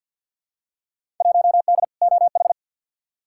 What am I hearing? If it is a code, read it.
Morse code, heard as a keyed tone, about 53 wpm, 700 Hz. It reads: 1D OH
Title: Field Day Exchanges - 50wpm